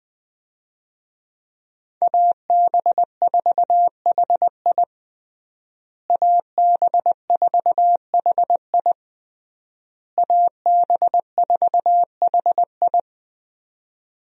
Morse code, heard as a keyed tone, about 21 wpm, 700 Hz. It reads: AB4HI AB4HI AB4HI
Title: USA Call Signs - 5 Characters - Repeated 3x - 20wpm